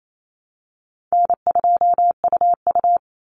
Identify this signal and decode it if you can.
Morse code, heard as a keyed tone, about 28 wpm, 700 Hz. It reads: N2UU
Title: Call Signs Repeated 3x - 28wpm